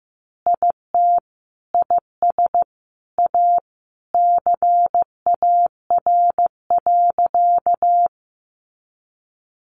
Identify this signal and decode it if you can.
Morse code, heard as a keyed tone, about 15 wpm, 700 Hz. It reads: IT IS A CAR.
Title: Easy Phrases 15wpm